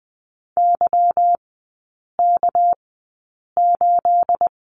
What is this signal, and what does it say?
Morse code, heard as a keyed tone, about 20 wpm, spaced as for 8 wpm, 690 Hz. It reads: YK8
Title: 212 Lesson 34e - Number 8 - Three Characters 20wpm